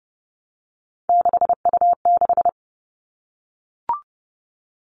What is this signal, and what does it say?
Morse code, heard as a keyed tone, about 30 wpm, 700 Hz. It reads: 6U6
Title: All Cut Numbers - 3-Digit 30wpm